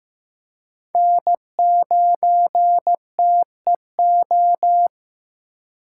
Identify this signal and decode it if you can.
Morse code, heard as a keyed tone, about 15 wpm, 700 Hz. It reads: N9TEO
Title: USA Call Signs - 5 Characters - 3x Character Spacing - 15wpm